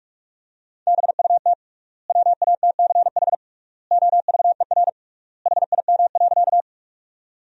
Morse code, heard as a keyed tone, about 43 wpm, 700 Hz. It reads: BUT WATCH OVER HIM.
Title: 3x Word-Spacing - Sentences from Top 300 Words - 45wpm